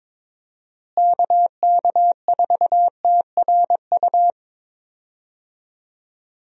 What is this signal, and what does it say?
Morse code, heard as a keyed tone, about 23 wpm, 700 Hz. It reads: KK4TRU
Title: USA Call Signs - 6 Characters - Repeated 3x - 22wpm